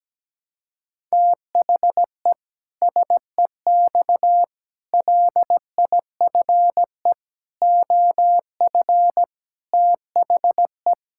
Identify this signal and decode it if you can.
Morse code, heard as a keyed tone, about 17 wpm, 700 Hz. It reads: THE SEX LIFE OF THE
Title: Pangram Challenge v2 - 17wpm